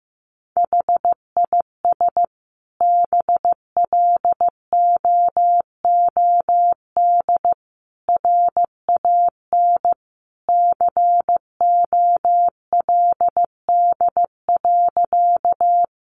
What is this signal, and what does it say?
Morse code, heard as a keyed tone, about 15 wpm, 700 Hz. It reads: HIS BLOOD RAN COLD.
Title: Sentences from Top 1100 Words 15wpm